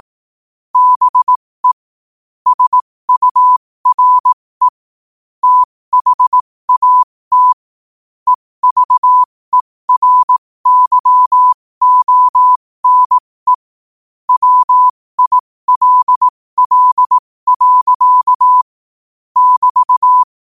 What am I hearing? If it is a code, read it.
Morse code, heard as a keyed tone, about 18 wpm, spaced as for 15 wpm, 1000 Hz. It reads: BE SURE THAT EVERYONE WILL. <BT>
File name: qotdfiles / 15wpm